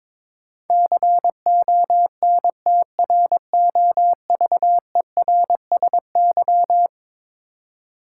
Morse code, heard as a keyed tone, about 22 wpm, 700 Hz. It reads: CONTROVERSY
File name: practice / 22wpm / Eleven Letter Words - 22wpm